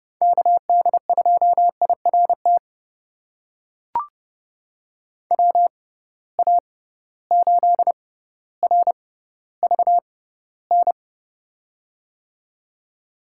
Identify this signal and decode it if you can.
Morse code, heard as a keyed tone, about 31 wpm, spaced as for 15 wpm, 700 Hz. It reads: KD2IRT WA8RVN
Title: USA Call Signs - 6 Characters - 6x Character Spacing - 30wpm